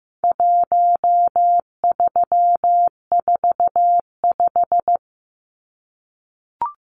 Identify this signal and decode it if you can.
Morse code, heard as a keyed tone, about 14 wpm, 700 Hz. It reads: 1345
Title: Numbers - 4-Digit 15wpm